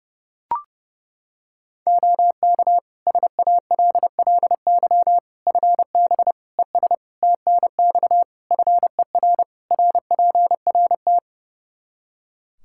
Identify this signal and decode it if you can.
Morse code, heard as a keyed tone, about 32 wpm, 700 Hz. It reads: OK SALLY FB ES TNX FER RPRT
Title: Practice QSO - Exchange 2 - Info - 30wpm